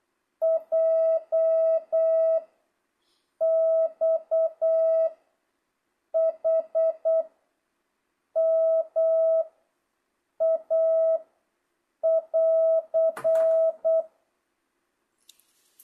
Morse code, heard as a keyed tone, about 8 wpm, 660 Hz. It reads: J X H M A <AR>
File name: jh8cbh / 2025cw